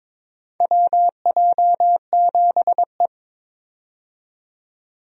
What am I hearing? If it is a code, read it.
Morse code, heard as a keyed tone, about 22 wpm, 700 Hz. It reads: WJ7E
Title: USA Call Signs - 4 Characters - 22wpm